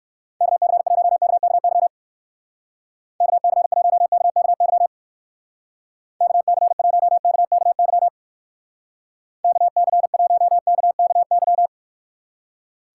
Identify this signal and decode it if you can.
Morse code, heard as a keyed tone, about 65 wpm, 700 Hz. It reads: KC1KKY KC1KKY KC1KKY KC1KKY